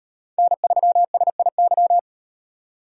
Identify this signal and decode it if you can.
Morse code, heard as a keyed tone, about 37 wpm, 700 Hz. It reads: N3SIY